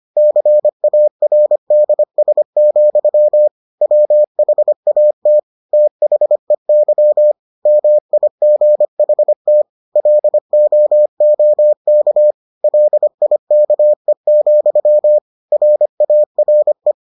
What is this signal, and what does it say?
Morse code, heard as a keyed tone, about 25 wpm, 600 Hz. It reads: CARDS, WHAT THEY MIGHT LOOK LIKE, RARE